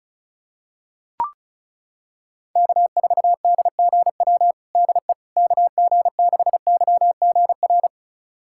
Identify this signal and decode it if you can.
Morse code, heard as a keyed tone, about 34 wpm, 700 Hz. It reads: K4DGW DE KG6YGR
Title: USA Callsign DE Callsign 35wpm